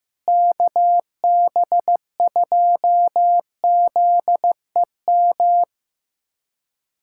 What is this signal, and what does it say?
Morse code, heard as a keyed tone, about 15 wpm, 700 Hz. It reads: KB2ZEM
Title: USA Callsign DE Callsign - 6 Characters - 15wpm